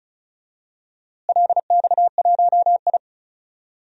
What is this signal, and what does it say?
Morse code, heard as a keyed tone, about 36 wpm, 700 Hz. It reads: LX1I
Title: DXCC-100 Call Signs - 4 Characters - Repeated 2x - 35wpm